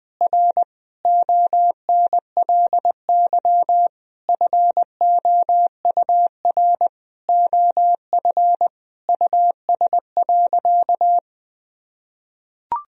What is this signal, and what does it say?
Morse code, heard as a keyed tone, about 20 wpm, 700 Hz. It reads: R ONLY FOUR OF US.
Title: Sentences from Top 300 Words and Abbreviations 20wpm